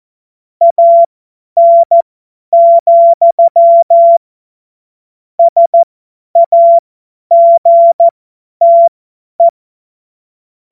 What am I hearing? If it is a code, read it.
Morse code, heard as a keyed tone, about 14 wpm, spaced as for 10 wpm, 680 Hz. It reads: AN, SAGTE
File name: Morsekurs / Witze / Witz 10. Mai 2022 50 F70